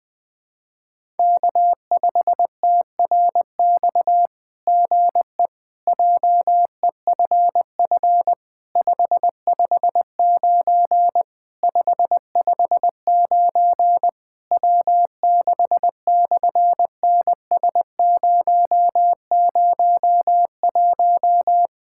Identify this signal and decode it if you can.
Morse code, heard as a keyed tone, about 20 wpm, 700 Hz. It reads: K5TRX GE JEFF 559 559 W6/NS001
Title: SOTA QSO Practice 20wpm